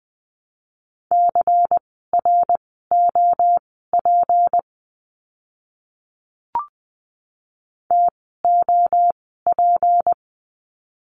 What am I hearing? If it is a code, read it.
Morse code, heard as a keyed tone, about 20 wpm, spaced as for 14 wpm, 700 Hz. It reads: CROP TOP
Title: 153 Lesson 25f - Letter p - Words 20wpm